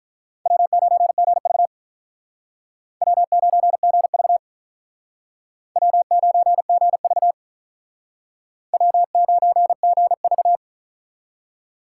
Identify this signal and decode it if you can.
Morse code, heard as a keyed tone, about 45 wpm, 700 Hz. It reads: W9GV W9GV W9GV W9GV